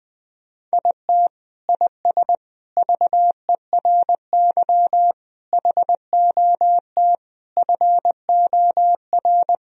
Morse code, heard as a keyed tone, about 20 wpm, 700 Hz. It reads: IT IS VERY HOT FOR
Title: Sentences from Top 1000 Words - Review Part 3 - 20wpm